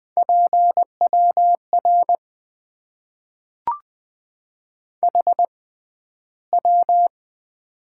Morse code, heard as a keyed tone, about 20 wpm, spaced as for 12 wpm, 700 Hz. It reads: PWR HW
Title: Top 90 QSO Elements - 6x Character Spacing - 20wpm